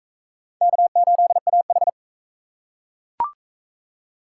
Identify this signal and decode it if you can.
Morse code, heard as a keyed tone, about 38 wpm, 700 Hz. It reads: K8AH